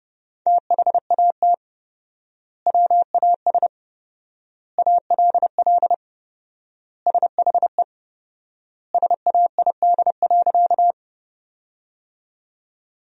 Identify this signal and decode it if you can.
Morse code, heard as a keyed tone, about 32 wpm, 700 Hz. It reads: THAT WAS ALL SHE SAID.
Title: 4x Word-Spacing - Sentences from Top 100 Words - 30wpm